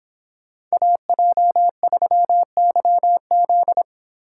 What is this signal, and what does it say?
Morse code, heard as a keyed tone, about 26 wpm, 700 Hz. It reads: AJ3YZ